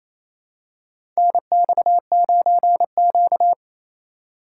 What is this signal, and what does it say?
Morse code, heard as a keyed tone, about 28 wpm, 700 Hz. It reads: NX9Q
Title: USA Call Signs - 4x Character Spacing - 28wpm